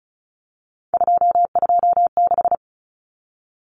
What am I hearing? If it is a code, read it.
Morse code, heard as a keyed tone, about 34 wpm, 700 Hz. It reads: 226
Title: Common Cut Numbers - 3-Digit 35wpm